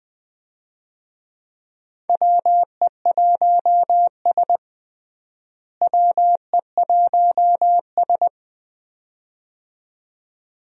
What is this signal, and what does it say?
Morse code, heard as a keyed tone, about 21 wpm, 700 Hz. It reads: WE1S WE1S
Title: USA Call Signs - 4 Characters - Repeated 2x - 20wpm